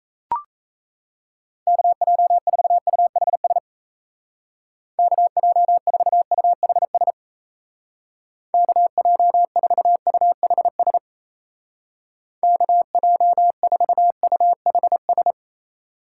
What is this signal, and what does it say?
Morse code, heard as a keyed tone, about 34 wpm, 700 Hz. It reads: KJ4UHS KJ4UHS KJ4UHS KJ4UHS
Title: Speed Racing - USA 2x3 Call Signs 28wpm